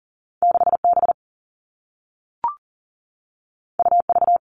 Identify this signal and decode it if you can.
Morse code, heard as a keyed tone, about 38 wpm, 700 Hz. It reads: 6B UV
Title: All Cut Numbers - 2-Digit 40wpm